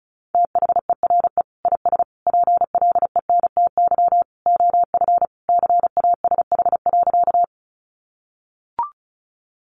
Morse code, heard as a keyed tone, about 34 wpm, 700 Hz. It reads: THERE IS PLENTY OF CASH.
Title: Sentences from Top 1300 Words 35wpm